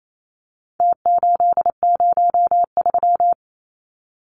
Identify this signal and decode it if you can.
Morse code, heard as a keyed tone, about 28 wpm, 700 Hz. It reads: T803
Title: Numbers - Contest Style - 4-Digit 28wpm